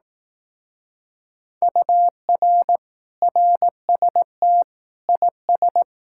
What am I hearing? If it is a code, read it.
Morse code, heard as a keyed tone, about 18 wpm, 700 Hz. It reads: UR RST IS
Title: Speed-Racing Common QSO Phrases Encore - 15wpm